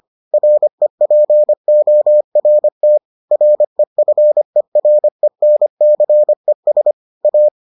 Morse code, heard as a keyed tone, about 25 wpm, 600 Hz. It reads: REPORT REFERENCES A